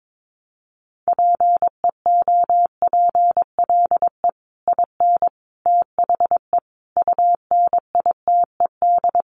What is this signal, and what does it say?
Morse code, heard as a keyed tone, about 22 wpm, 700 Hz. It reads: PEOPLE IN THE UNITED
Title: Sets of 5 Words Encore 22wpm